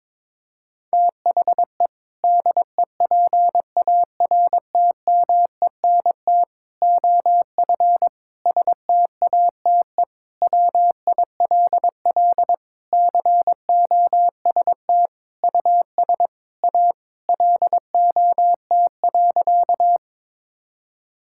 Morse code, heard as a keyed tone, about 22 wpm, 700 Hz. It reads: THE DEPARTMENT OF STATE WILL COST US A LOT.
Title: Sentences from Top 1000 Words - Review Part 5 - 22wpm